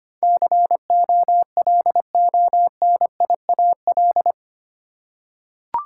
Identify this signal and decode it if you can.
Morse code, heard as a keyed tone, about 25 wpm, 700 Hz. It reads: COLONIAL
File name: 8 Letter Words - First 3 Letters Given - 25wpm